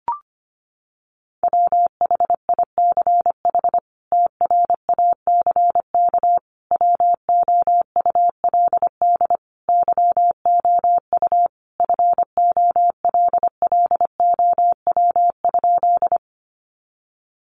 Morse code, heard as a keyed tone, about 25 wpm, 690 Hz. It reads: WHICH TRACK WOULD YOU FOLLOW?
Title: Sentences from Top 1100 Words 25wpm